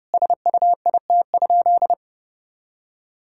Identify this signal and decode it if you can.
Morse code, heard as a keyed tone, about 30 wpm, 700 Hz. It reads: SUIT?
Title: Sentences from Top 1600 Words 30wpm